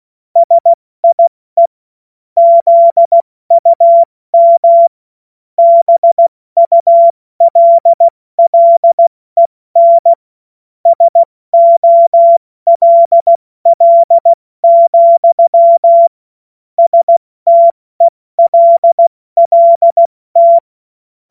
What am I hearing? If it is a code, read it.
Morse code, heard as a keyed tone, about 16 wpm, 680 Hz. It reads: SIE ZUM BULLEN SOLL, STELLT
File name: Witz-22-Juli-2025-70-F80